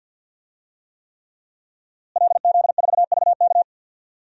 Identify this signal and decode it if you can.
Morse code, heard as a keyed tone, about 47 wpm, 700 Hz. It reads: LZ4VK
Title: DXCC-100 Call Signs - 5 Characters - Repeated 3x - 50wpm